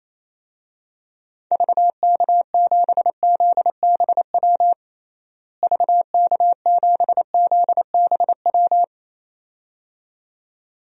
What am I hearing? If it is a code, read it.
Morse code, heard as a keyed tone, about 28 wpm, 700 Hz. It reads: VK7ZBW VK7ZBW
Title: DXCC-100 Call Signs - 6 Characters - Repeated 2x - 28wpm